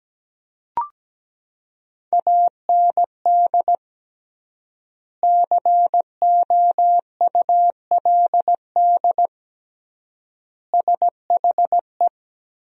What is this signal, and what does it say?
Morse code, heard as a keyed tone, about 17 wpm, 700 Hz. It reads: AND COULD SHE
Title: 3x Word-Spacing - Sentences from Top 100 Words - 17wpm